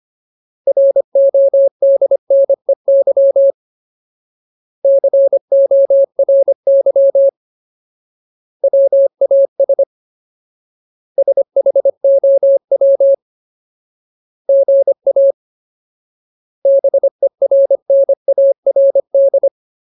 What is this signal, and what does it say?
Morse code, heard as a keyed tone, about 25 wpm, 550 Hz. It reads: RODNEY CORY WAS SHOW GA BERNARD
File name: Ultimate Word List / 25WPM Ultimate